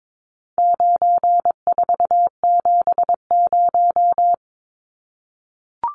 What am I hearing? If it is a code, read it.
Morse code, heard as a keyed tone, about 22 wpm, 700 Hz. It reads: 9470
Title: Numbers - 4-Digit 22wpm